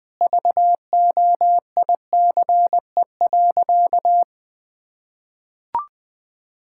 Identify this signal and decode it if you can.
Morse code, heard as a keyed tone, about 20 wpm, 700 Hz. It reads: VOICE.
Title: Sentences from Top 1600 Words 20wpm